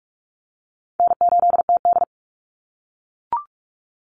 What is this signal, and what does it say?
Morse code, heard as a keyed tone, about 44 wpm, 700 Hz. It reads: N8TD
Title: Call Signs - 45wpm